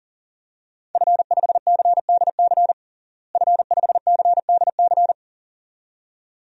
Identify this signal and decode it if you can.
Morse code, heard as a keyed tone, about 40 wpm, 700 Hz. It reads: F5CDC F5CDC
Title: DXCC-100 Call Signs - 5 Characters - Repeated 2x - 40wpm